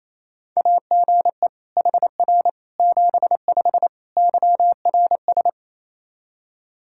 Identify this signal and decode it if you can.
Morse code, heard as a keyed tone, about 28 wpm, 700 Hz. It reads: AGE HR 75 YRS
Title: K1USN SST QSO Practice 28wpm